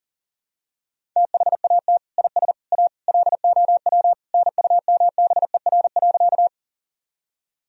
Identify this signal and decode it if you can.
Morse code, heard as a keyed tone, about 40 wpm, 700 Hz. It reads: THAT IS A LOW NUMBER.
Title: Sentences from Top 1000 Words - Review Part 1 - 40wpm